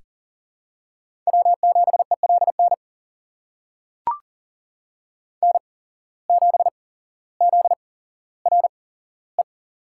Morse code, heard as a keyed tone, about 40 wpm, spaced as for 17 wpm, 700 Hz. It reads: W7ELN N7ZRE
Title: USA Call Signs - 5 Characters - 8x Character Spacing - 40wpm